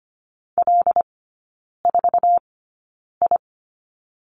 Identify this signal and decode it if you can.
Morse code, heard as a keyed tone, about 25 wpm, spaced as for 9 wpm, 700 Hz. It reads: L4I
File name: Lesson 17e - Letter u - Three Characters 25wpm